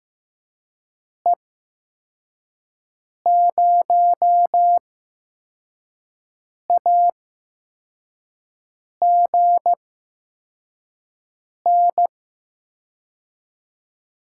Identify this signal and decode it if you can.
Morse code, heard as a keyed tone, about 15 wpm, spaced as for 4 wpm, 700 Hz. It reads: E0AGN
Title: USA Call Signs - 6 Characters - 8x Character Spacing - 15wpm